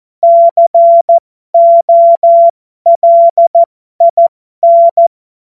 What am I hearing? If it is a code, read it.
Morse code, heard as a keyed tone, about 14 wpm, spaced as for 12 wpm, 680 Hz. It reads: COLIN
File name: Morsekurs / Witze